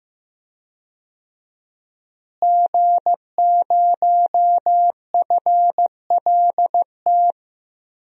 Morse code, heard as a keyed tone, about 15 wpm, 700 Hz. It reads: G0FLT
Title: DXCC-100 Call Signs - 5 Characters - Repeated 2x - 15wpm